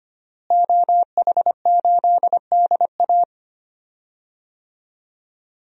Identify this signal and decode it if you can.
Morse code, heard as a keyed tone, about 25 wpm, 700 Hz. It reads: OH8DA
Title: DXCC-100 Call Signs - 5 Characters - Repeated 2x - 25wpm